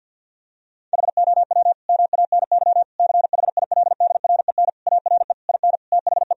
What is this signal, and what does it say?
Morse code, heard as a keyed tone, about 50 wpm, 700 Hz. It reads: HOW MANY CHILDREN ARE IN THE
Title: Sentences from Top 1000 Words - Review Part 4 - 50wpm